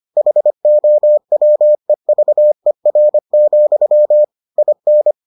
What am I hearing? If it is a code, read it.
Morse code, heard as a keyed tone, about 25 wpm, 600 Hz. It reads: HOWEVER, IN